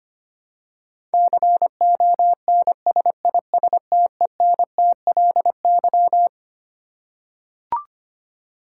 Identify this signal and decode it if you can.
Morse code, heard as a keyed tone, about 25 wpm, 700 Hz. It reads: CONSISTENTLY
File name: Twelve Letter Words - 25wpm